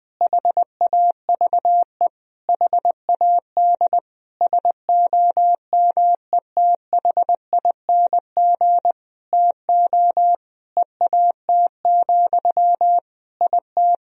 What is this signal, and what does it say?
Morse code, heard as a keyed tone, about 20 wpm, 700 Hz. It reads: HAVE HAD SOMETHING TO EAT, IT
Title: Sentences from Top 1000 Words - Review Part 1 - 20wpm